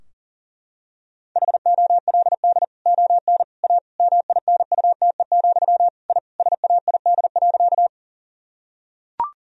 Morse code, heard as a keyed tone, about 40 wpm, 700 Hz. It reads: HOLD ON A MINUTE, I SAID.
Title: Sentences from Top 1000 Words - Review Part 2 - 40wpm